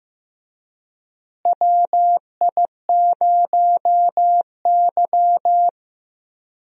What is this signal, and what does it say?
Morse code, heard as a keyed tone, about 15 wpm, 700 Hz. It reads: WI0Y
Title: USA Call Signs - 4 Characters - Repeated 2x - 15wpm